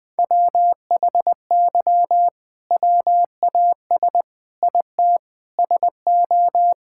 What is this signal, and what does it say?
Morse code, heard as a keyed tone, about 20 wpm, 700 Hz. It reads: WHY WAS IT SO
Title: Sentences from Top 1000 Words - Review Part 5 - 20wpm